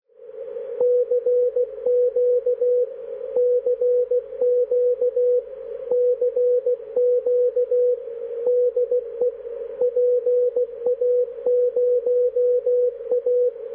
Morse code, heard as a keyed tone, about 16 wpm, 490 Hz, with radio noise. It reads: CQ CQ CQ DE PA0A
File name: PA0A- -1856z-502.5kHz1